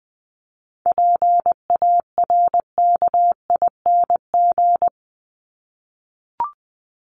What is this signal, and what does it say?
Morse code, heard as a keyed tone, about 20 wpm, 700 Hz. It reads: PARKING
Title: Top 1501-1750 Words 20wpm